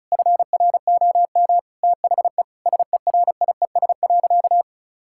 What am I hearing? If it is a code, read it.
Morse code, heard as a keyed tone, about 36 wpm, 700 Hz. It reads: FROM THE SERIES.
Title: Sentences from Top 1000 Words - Review Part 5 - 35wpm